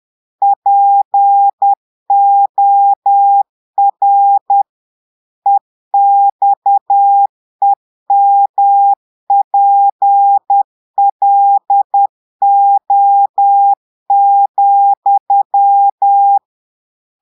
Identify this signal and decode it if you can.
Morse code, heard as a keyed tone, about 10 wpm, 800 Hz. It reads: POR EXEMPLO,
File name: portugais texte en clair 2 vitesse 10 mots minutes